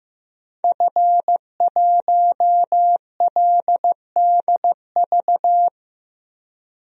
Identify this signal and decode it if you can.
Morse code, heard as a keyed tone, about 15 wpm, 700 Hz. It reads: F1LDV
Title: DXCC-100 Call Signs - 5 Characters - 15wpm